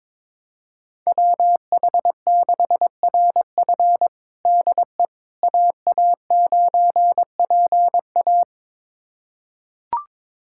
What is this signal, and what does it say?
Morse code, heard as a keyed tone, about 21 wpm, 700 Hz. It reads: WH6RF DE AA9PA
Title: USA Callsign DE Callsign - 5 Characters - 22wpm